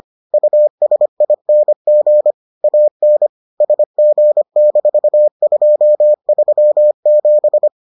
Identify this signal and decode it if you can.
Morse code, heard as a keyed tone, about 25 wpm, 600 Hz. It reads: USING AN SG-237